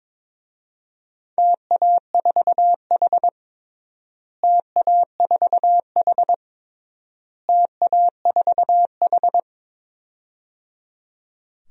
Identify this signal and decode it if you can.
Morse code, heard as a keyed tone, about 22 wpm, 700 Hz. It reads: TA4H TA4H TA4H
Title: DXCC-100 Call Signs - 4 Characters - Repeated 3x - 22wpm